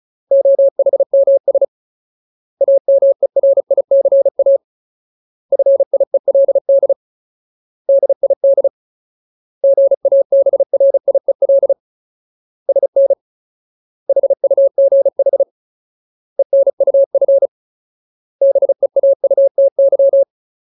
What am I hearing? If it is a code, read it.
Morse code, heard as a keyed tone, about 34 wpm, 550 Hz. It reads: OHMS AMERICA FIELD DID GABRIEL SN HUGH ENUF BEAUTY
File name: Ultimate Word List / 35WPM Ultimate